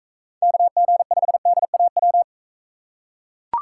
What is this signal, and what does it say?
Morse code, heard as a keyed tone, about 41 wpm, 700 Hz. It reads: KG5DAW